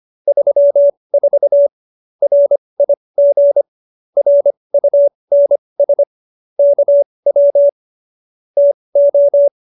Morse code, heard as a keyed tone, about 25 wpm, spaced as for 20 wpm, 570 Hz. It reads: 34 RIG RUNS KW TO
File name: QSO307 20